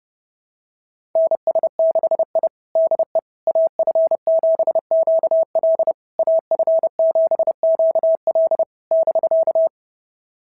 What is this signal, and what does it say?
Morse code, heard as a keyed tone, about 28 wpm, 690 Hz. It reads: NS6I DE AF7QL AF7QL <BK>
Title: SOTA QSO Practice 30wpm